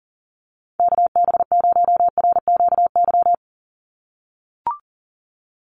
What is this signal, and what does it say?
Morse code, heard as a keyed tone, about 37 wpm, 700 Hz. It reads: KB0RQY
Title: USA Call Signs 40wpm